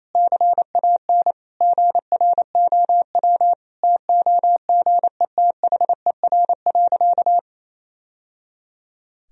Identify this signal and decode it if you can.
Morse code, heard as a keyed tone, about 28 wpm, 700 Hz. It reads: CAN GROW TOGETHER.
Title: Sentences from Top 1000 Words - Review Part 1 - 28wpm